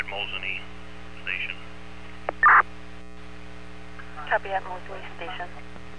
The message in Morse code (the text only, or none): none